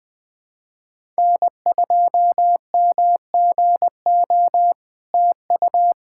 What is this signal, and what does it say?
Morse code, heard as a keyed tone, about 20 wpm, 700 Hz. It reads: N2MGO TU
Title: SOTA QSO Practice 20wpm